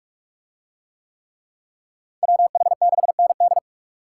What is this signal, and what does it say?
Morse code, heard as a keyed tone, about 44 wpm, 700 Hz. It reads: WH6ND